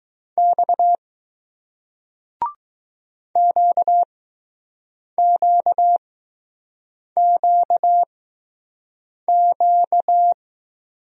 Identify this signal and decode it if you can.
Morse code, heard as a keyed tone, about 20 wpm, spaced as for 6 wpm, 700 Hz. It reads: X QQQQ